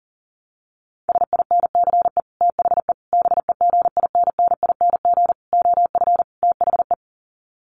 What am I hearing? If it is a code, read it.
Morse code, heard as a keyed tone, about 40 wpm, 700 Hz. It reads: SINCE THE BEGINNING OF THE
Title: Sets of 5 Words Encore2 40wpm